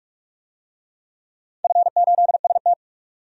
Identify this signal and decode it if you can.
Morse code, heard as a keyed tone, about 44 wpm, 700 Hz. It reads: F8ST
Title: DXCC-100 Call Signs - 4 Characters - 45wpm